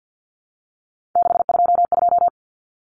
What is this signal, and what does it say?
Morse code, heard as a keyed tone, about 48 wpm, 700 Hz. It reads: 622
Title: Common Cut Numbers - 3-Digit 50wpm